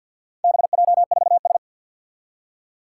Morse code, heard as a keyed tone, about 50 wpm, 700 Hz. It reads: BJ4S
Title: Four Challenging Letter-Number Combinations 50wpm